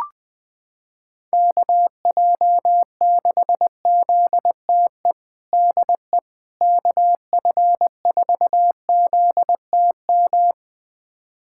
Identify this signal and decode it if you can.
Morse code, heard as a keyed tone, about 20 wpm, 700 Hz. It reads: KJ6ZTE DE KF4ZTM
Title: USA Callsign DE Callsign - 6 Characters - 20wpm